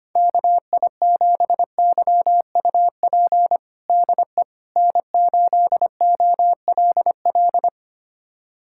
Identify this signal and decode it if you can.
Morse code, heard as a keyed tone, about 25 wpm, 700 Hz. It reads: KI7YUP DE N8OLL
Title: USA Callsign DE Callsign 25wpm